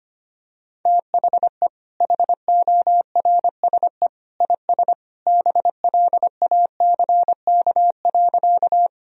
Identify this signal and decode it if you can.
Morse code, heard as a keyed tone, about 25 wpm, 700 Hz. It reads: THE HORSE IS BLACK.
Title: Sentences from Top 1000 Words - Review Part 2 - 25wpm